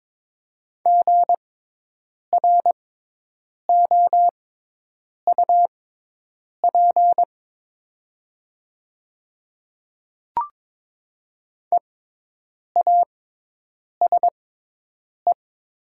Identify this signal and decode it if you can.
Morse code, heard as a keyed tone, about 22 wpm, spaced as for 8 wpm, 700 Hz. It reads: GROUP EASE
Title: 6x Character Spacing - Top 300 Words - 22wpm